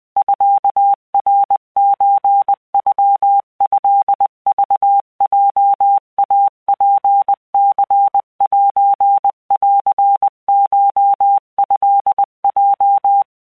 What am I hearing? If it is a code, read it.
Morse code (Japanese Wabun code), heard as a keyed tone, about 20 wpm, 800 Hz. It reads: ミナソノトクヲイツニセンコトヲ